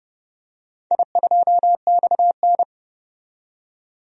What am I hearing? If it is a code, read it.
Morse code, heard as a keyed tone, about 30 wpm, 700 Hz. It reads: I2XN
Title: DXCC-100 Call Signs - 4 Characters - 30wpm